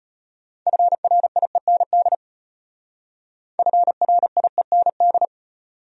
Morse code, heard as a keyed tone, about 31 wpm, 700 Hz. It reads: FRIEND FRIEND
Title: Speed-Racing Top 500 Words 25wpm